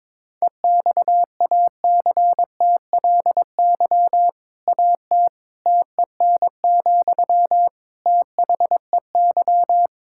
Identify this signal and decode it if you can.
Morse code, heard as a keyed tone, about 22 wpm, 700 Hz. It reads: EXACTLY AT TEN, THEY
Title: Sentences from Top 1000 Words - Review Part 6 - 22wpm